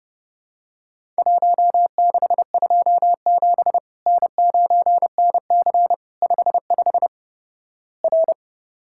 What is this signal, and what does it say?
Morse code, heard as a keyed tone, about 30 wpm, 700 Hz. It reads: 1627 N9NC 55 R
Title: Worked All Europe DX Contest - QTC Exchange - 30wpm